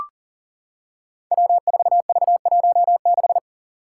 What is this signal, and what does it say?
Morse code, heard as a keyed tone, about 38 wpm, 700 Hz. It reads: W4V16
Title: Five Challenging Letter-Number Combinations 40wpm